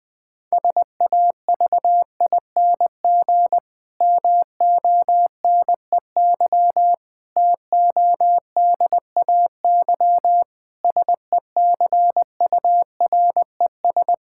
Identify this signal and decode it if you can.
Morse code, heard as a keyed tone, about 20 wpm, 700 Hz. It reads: SAVING MONEY TODAY SECURES
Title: Difficult Sentences with Hints - 20wpm